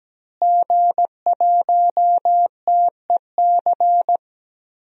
Morse code, heard as a keyed tone, about 17 wpm, 700 Hz. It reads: G1TEC
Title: DXCC-100 Call Signs - 5 Characters - Repeated 3x - 17wpm